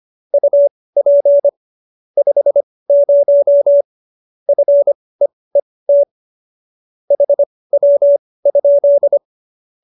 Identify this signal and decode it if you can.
Morse code, heard as a keyed tone, about 25 wpm, spaced as for 18 wpm, 570 Hz. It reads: UP 50 FEET HW?